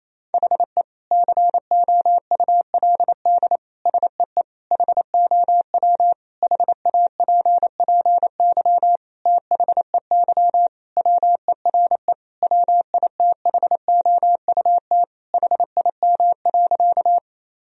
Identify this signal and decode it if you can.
Morse code, heard as a keyed tone, about 27 wpm, 700 Hz. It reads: HE COULD SEE HOW HAPPY THEY WERE WITHOUT HIM.
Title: Sentences from Top 1000 Words - Review Part 3 - 28wpm